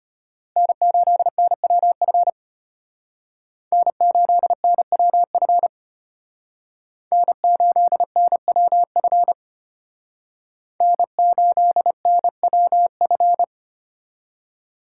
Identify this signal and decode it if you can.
Morse code, heard as a keyed tone, about 32 wpm, 700 Hz. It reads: N8NWF N8NWF N8NWF N8NWF